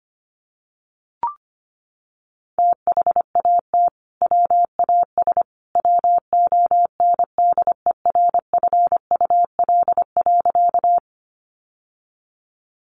Morse code, heard as a keyed tone, about 25 wpm, 700 Hz. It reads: THAT WAS WONDERFUL.
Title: Sentences from Top 1100 Words 25wpm